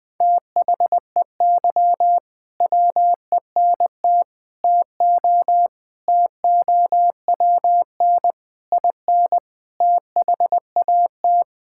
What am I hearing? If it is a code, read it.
Morse code, heard as a keyed tone, about 21 wpm, 700 Hz. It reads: THEY WENT TO TOWN IN THAT
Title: Sentences from Top 1000 Words - Review Part 5 - 20wpm